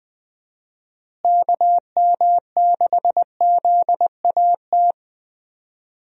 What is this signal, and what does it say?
Morse code, heard as a keyed tone, about 21 wpm, 700 Hz. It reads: KM6ZAT
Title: USA Call Signs - 6 Characters - Repeated 2x - 20wpm